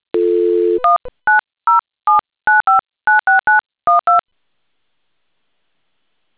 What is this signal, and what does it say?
DTMF, heard as telephone touch tones, tones of about 150 ms. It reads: 19079596912